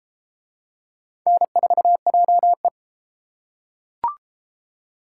Morse code, heard as a keyed tone, about 28 wpm, 700 Hz. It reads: N4JE